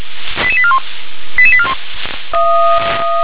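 Five-tone selective call (ZVEI), heard as quick runs of five tones, tones of about 70 ms. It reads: 88841 88841
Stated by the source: ZVEI five-tone call received by SDR.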